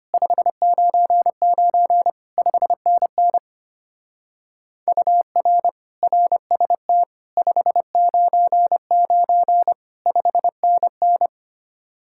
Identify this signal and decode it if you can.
Morse code, heard as a keyed tone, about 32 wpm, 700 Hz. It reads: 599 5NN UR RST 599 5NN